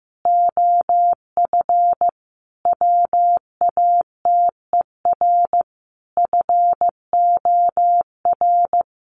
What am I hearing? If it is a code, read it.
Morse code, heard as a keyed tone, about 15 wpm, 700 Hz. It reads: OF WATER FOR
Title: Sentences from Top 1300 Words 15wpm